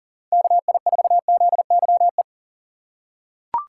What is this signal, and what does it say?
Morse code, heard as a keyed tone, about 38 wpm, 700 Hz. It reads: KI4ZYE
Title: USA Call Signs - 6 Characters - 6x Character Spacing - 40wpm